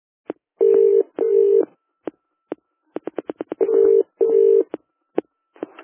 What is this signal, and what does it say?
Ringback tone (UK), 400 + 450 Hz.